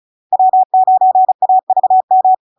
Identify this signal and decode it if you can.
Morse code, heard as a keyed tone, about 36 wpm, 750 Hz. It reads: W9AVM
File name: LARK-FD-2022-Log-in-CW-35wpm-750-Hz